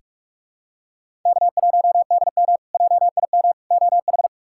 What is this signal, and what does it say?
Morse code, heard as a keyed tone, about 46 wpm, 700 Hz. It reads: K1DM JIM OH
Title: K1USN SST QSO Practice - 45wpm